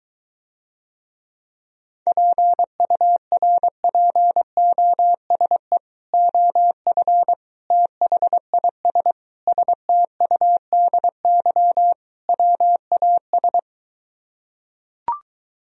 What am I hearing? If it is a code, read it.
Morse code, heard as a keyed tone, about 23 wpm, 700 Hz. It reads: PURPOSE OF THIS STUDY WAS